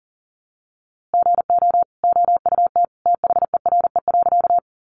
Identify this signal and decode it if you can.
Morse code, heard as a keyed tone, about 37 wpm, 700 Hz. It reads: GO OUT THERE.
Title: A Sound Of Thunder 40wpm